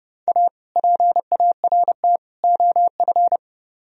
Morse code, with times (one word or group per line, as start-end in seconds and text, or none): 0.3-0.5 A
0.8-2.2 PART
2.4-3.4 OF